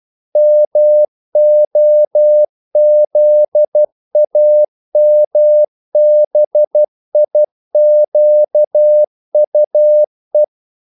Morse code, heard as a keyed tone, about 12 wpm, 600 Hz. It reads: MOZAMBIQUE
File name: CW / Practice / NationalCapitals12